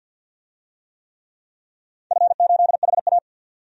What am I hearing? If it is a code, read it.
Morse code, heard as a keyed tone, about 47 wpm, 700 Hz. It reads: F8HA
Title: DXCC-100 Call Signs - 4 Characters - 50wpm